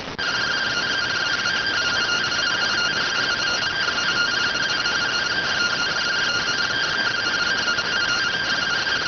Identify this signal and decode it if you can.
Two-tone teleprinter signal, about 170 Hz shift.